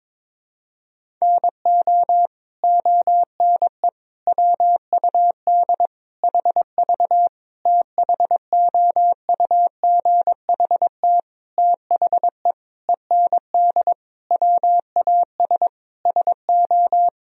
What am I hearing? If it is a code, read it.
Morse code, heard as a keyed tone, about 22 wpm, 700 Hz. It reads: NO ONE WUD HV THOUGHT THE END WAS SO
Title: Sentences from Top 300 Words and Abbreviations 22wpm